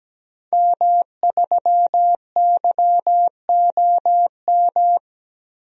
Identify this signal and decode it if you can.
Morse code, heard as a keyed tone, about 17 wpm, 700 Hz. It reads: M3YOM
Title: DXCC-100 Call Signs - 5 Characters - Repeated 3x - 17wpm